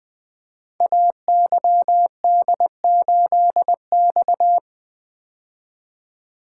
Morse code, heard as a keyed tone, about 20 wpm, 700 Hz. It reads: AYD8X
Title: Code Groups - 5 Letter:Numbers - 20wpm